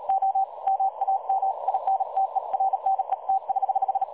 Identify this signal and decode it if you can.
Morse code, heard as a keyed tone, about 36 wpm, 770 Hz, with radio noise. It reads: O MAM INTERNET<HH>
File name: dlazaawansowanych / nagraniazpasma / websdr recording 3565.0kHz